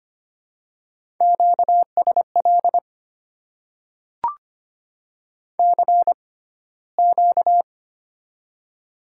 Morse code, heard as a keyed tone, about 25 wpm, spaced as for 15 wpm, 690 Hz. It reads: QSL CQ
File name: Top 90 QSO Elements - 6x Character Spacing - 25wpm